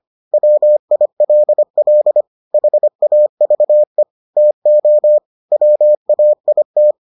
Morse code, heard as a keyed tone, about 25 wpm, 600 Hz. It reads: WILL HAVE TO WAIT